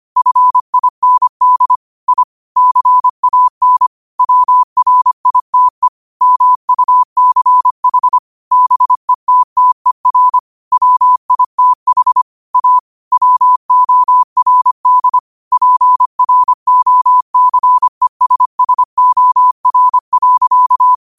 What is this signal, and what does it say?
Morse code, heard as a keyed tone, about 25 wpm, 1000 Hz. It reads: FIND I CAN WRITE MUCH BETTER WITH A WORD PROCESSOR.